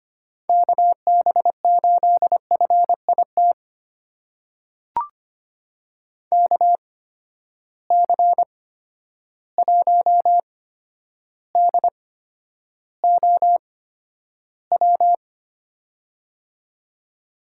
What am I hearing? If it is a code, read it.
Morse code, heard as a keyed tone, about 25 wpm, spaced as for 11 wpm, 700 Hz. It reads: KB8FIT KC1DOW
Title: USA Call Signs - 6 Characters - 8x Character Spacing - 25wpm